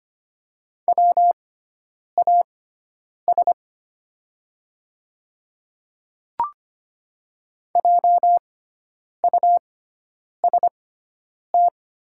Morse code, heard as a keyed tone, about 25 wpm, spaced as for 9 wpm, 700 Hz. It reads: WAS JUST
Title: 6x Character Spacing - Top 300 Words - 25wpm